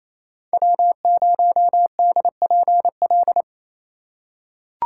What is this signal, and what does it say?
Morse code, heard as a keyed tone, about 28 wpm, 700 Hz. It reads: W0DPL
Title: USA Call Signs - 5 Characters - 4x Character Spacing - 28wpm